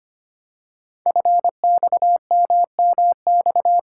Morse code, heard as a keyed tone, about 25 wpm, 700 Hz. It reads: FXMMX
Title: Code Groups - 5 Letters - 25wpm